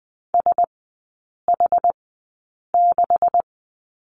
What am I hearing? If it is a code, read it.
Morse code, heard as a keyed tone, about 20 wpm, spaced as for 8 wpm, 700 Hz. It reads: SH6
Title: 178 Lesson 29e - Slash - Three Characters 20wpm